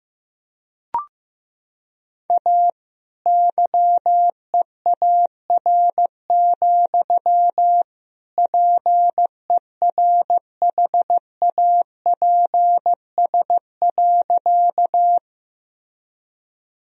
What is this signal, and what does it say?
Morse code, heard as a keyed tone, about 15 wpm, 700 Hz. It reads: A YEAR, PERHAPS.
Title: Sentences from Top 1000 Words - Review Part 1 - 15wpm